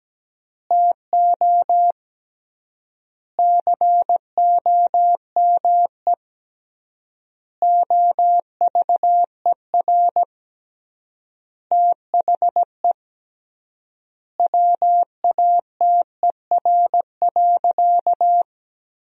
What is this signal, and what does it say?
Morse code, heard as a keyed tone, about 17 wpm, 700 Hz. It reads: TO COME OVER THE WATER.
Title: 3x Word-Spacing - Sentences from Top 300 Words - 17wpm